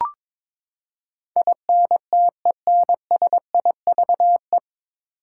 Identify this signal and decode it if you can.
Morse code, heard as a keyed tone, about 22 wpm, 700 Hz. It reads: INTENSIVE
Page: Morse Code Ninja practice 22wpm